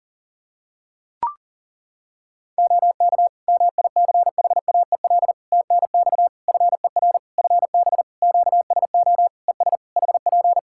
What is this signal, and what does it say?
Morse code, heard as a keyed tone, about 40 wpm, 700 Hz. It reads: OK MICHAEL TNX FER FB QSO ES HP
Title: Practice QSO - Exchange 4 - Ending - 40wpm